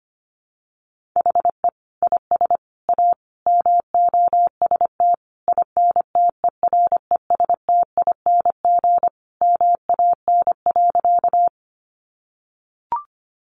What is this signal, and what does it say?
Morse code, heard as a keyed tone, about 25 wpm, 700 Hz. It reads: HE IS A MOST INTERESTING MAN.
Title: Sentences from Top 1000 Words 25wpm